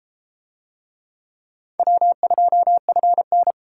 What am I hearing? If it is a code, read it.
Morse code, heard as a keyed tone, about 33 wpm, 700 Hz. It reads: W2FN